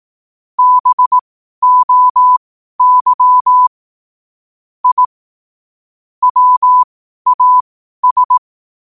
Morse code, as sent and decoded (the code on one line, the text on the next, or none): -... --- -.-- / .. / .-- .- ...
BOY I WAS